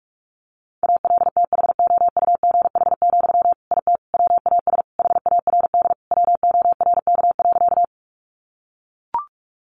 Morse code, heard as a keyed tone, about 44 wpm, 700 Hz. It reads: ALTHOUGH, IT WAS HARD WORK.
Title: Sentences from Top 600 Words 45wpm